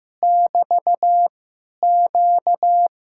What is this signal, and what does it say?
Morse code, heard as a keyed tone, about 15 wpm, 700 Hz. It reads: <BT> Q
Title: Speed-Racing Common QSO Phrases Encore - 15wpm